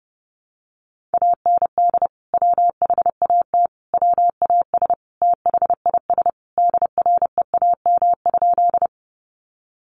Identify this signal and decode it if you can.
Morse code, heard as a keyed tone, about 28 wpm, 700 Hz. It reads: AND WHAT WAS THIS DREAM?
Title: Sentences from Top 1500 Words 30wpm